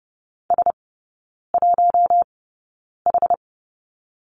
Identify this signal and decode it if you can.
Morse code, heard as a keyed tone, about 30 wpm, spaced as for 9 wpm, 700 Hz. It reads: S1H
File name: Lesson 17e - Letter u - Three Characters 30wpm